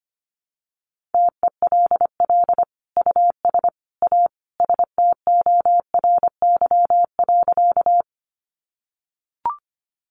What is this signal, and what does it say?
Morse code, heard as a keyed tone, about 25 wpm, 700 Hz. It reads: TELL US A STORY.
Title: Sentences from Top 300 Words 25wpm